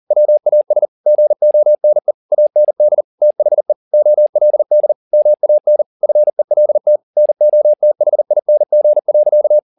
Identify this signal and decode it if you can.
Morse code, heard as a keyed tone, about 40 wpm, 600 Hz. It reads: WAS GONE AND THE OLD MAN FELT NOTHING.